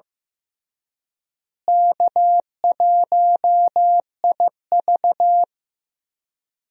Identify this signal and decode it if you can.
Morse code, heard as a keyed tone, about 16 wpm, 700 Hz. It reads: K1IV
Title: USA Call Signs - 4 Characters - Repeated 3x - 15wpm